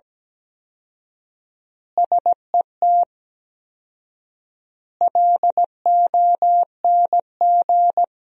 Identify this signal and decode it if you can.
Morse code, heard as a keyed tone, about 17 wpm, 700 Hz. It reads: SET LONG
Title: 4x Word-Spacing - Sentences from Top 300 Words - 17wpm